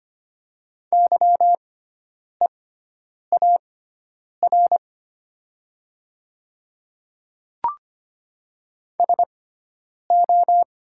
Morse code, heard as a keyed tone, about 25 wpm, spaced as for 9 wpm, 700 Hz. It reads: YEAR SO
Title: 6x Character Spacing - Top 200 Words - 25wpm